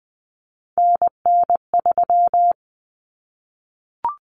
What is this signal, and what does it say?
Morse code, heard as a keyed tone, about 20 wpm, 700 Hz. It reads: NN3
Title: Common Cut Numbers - 3-Digit 20wpm